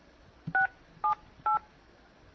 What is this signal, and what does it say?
DTMF, heard as telephone touch tones, tones of about 130 ms. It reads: 678